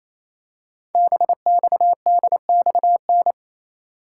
Morse code, heard as a keyed tone, about 27 wpm, 700 Hz. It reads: BXDXN
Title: Code Groups - 5 Letter:Numbers - 28wpm